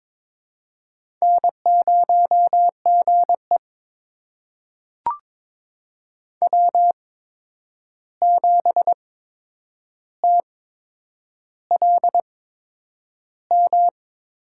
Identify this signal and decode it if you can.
Morse code, heard as a keyed tone, about 23 wpm, spaced as for 9 wpm, 700 Hz. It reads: N0GE W7TLM
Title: USA Call Signs - 8x Character Spacing - 22wpm